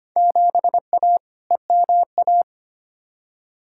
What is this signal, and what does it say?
Morse code, heard as a keyed tone, about 25 wpm, 700 Hz. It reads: 7A EMA
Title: Field Day Exchanges Encore - 25wpm